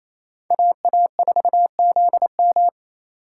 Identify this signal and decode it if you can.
Morse code, heard as a keyed tone, about 34 wpm, 700 Hz. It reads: AA4ZM
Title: Speed Racing - USA Call Signs 28wpm